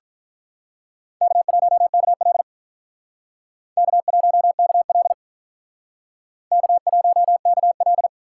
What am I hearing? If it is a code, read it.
Morse code, heard as a keyed tone, about 45 wpm, 700 Hz. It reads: K1KL K1KL K1KL